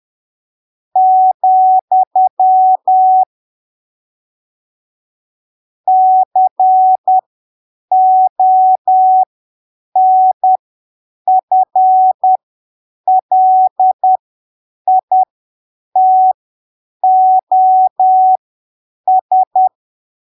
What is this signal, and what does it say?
Morse code, heard as a keyed tone, about 10 wpm, spaced as for 7 wpm, 750 Hz. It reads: , CONFLITOS
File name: arquivos / texto / texto 30 10ppm